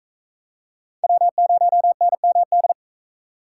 Morse code, heard as a keyed tone, about 44 wpm, 700 Hz. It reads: W0NMD